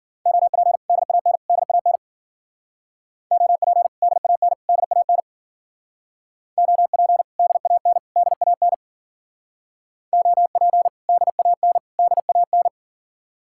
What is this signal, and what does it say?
Morse code, heard as a keyed tone, about 49 wpm, 700 Hz. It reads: OP DAN DAN OP DAN DAN OP DAN DAN OP DAN DAN
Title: Speed-Racing Common QSO Phrases - 40wpm